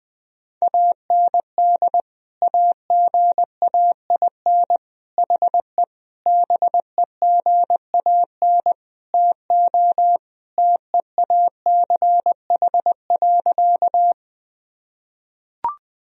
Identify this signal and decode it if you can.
Morse code, heard as a keyed tone, about 20 wpm, 700 Hz. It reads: AND AGAIN HE BEGAN TO TEACH.
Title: Sentences from Top 1000 Words - Review Part 2 - 20wpm